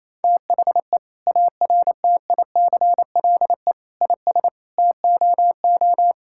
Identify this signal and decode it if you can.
Morse code, heard as a keyed tone, about 28 wpm, 700 Hz. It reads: THE ARTICLE IS TOO
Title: Sentences from Top 1000 Words - Review Part 5 - 28wpm